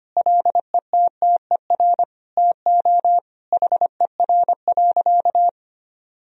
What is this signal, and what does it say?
Morse code, heard as a keyed tone, about 25 wpm, 700 Hz. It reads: LETTER TO HER.
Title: Sentences from Top 1000 Words - Review Part 3 - 25wpm